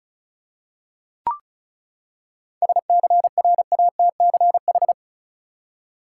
Morse code, heard as a keyed tone, about 34 wpm, 700 Hz. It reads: SCRATCH
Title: Seven Letter Words - 35wpm